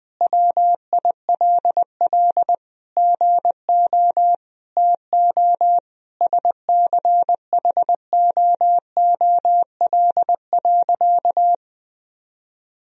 Morse code, heard as a keyed tone, about 20 wpm, 700 Hz. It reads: WILL GO TO SCHOOL.
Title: Sentences from Top 1000 Words - Review Part 1 - 20wpm